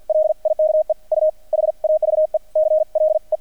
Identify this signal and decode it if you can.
Morse code, heard as a keyed tone, about 50 wpm, spaced as for 43 wpm, 620 Hz. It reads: LIKE U HAVE CLE
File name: archived / CRSnet